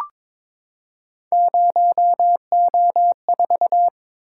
Morse code, heard as a keyed tone, about 22 wpm, 700 Hz. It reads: 0O4
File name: practice / 22wpm / Three Letter-Number Combinations - 22wpm